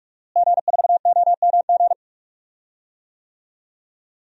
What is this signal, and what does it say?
Morse code, heard as a keyed tone, about 42 wpm, 700 Hz. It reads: G4OMG
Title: DXCC-100 Call Signs - 5 Characters - Repeated 3x - 45wpm